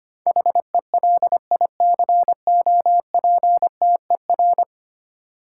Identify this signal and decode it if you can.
Morse code, heard as a keyed tone, about 25 wpm, 700 Hz. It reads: HELICOPTER
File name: Ten Letter Words - 25wpm